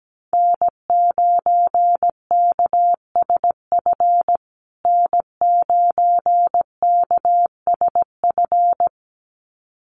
Morse code, heard as a keyed tone, about 17 wpm, 700 Hz. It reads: N9KSF N9KSF
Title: USA Call Signs Repeated 2x 17wpm